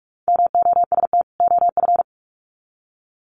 Morse code, heard as a keyed tone, about 44 wpm, 700 Hz. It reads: MOST OF
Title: Sets of 2 Words 45wpm